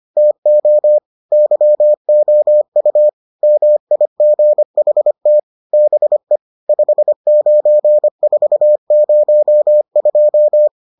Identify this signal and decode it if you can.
Morse code, heard as a keyed tone, about 25 wpm, 600 Hz. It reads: TO YOU MIGHT BE 59402